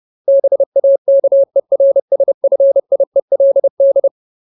Morse code, heard as a keyed tone, about 30 wpm, 550 Hz. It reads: BAKERSFIELD
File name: Ultimate Word List / 30WPM Ultimate